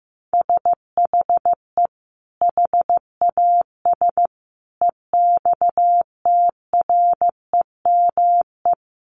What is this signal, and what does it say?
Morse code, heard as a keyed tone, about 15 wpm, 700 Hz. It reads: SHE HAS EXTREME
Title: Sentences from Top 1300 Words 15wpm